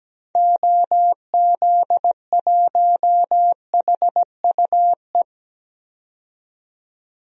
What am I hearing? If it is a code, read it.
Morse code, heard as a keyed tone, about 17 wpm, 700 Hz. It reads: OZ1HUE
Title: DXCC-100 Call Signs - 6 Characters - 17wpm